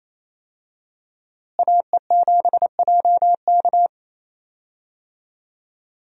Morse code, heard as a keyed tone, about 28 wpm, 700 Hz. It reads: AE7JK
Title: USA Call Signs - 5 Characters - Repeated 2x - 28wpm